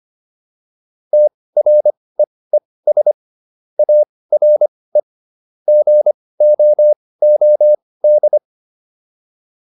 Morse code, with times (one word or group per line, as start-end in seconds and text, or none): 1.1-3.1 TREES
3.8-5.0 ARE
5.7-8.4 GOOD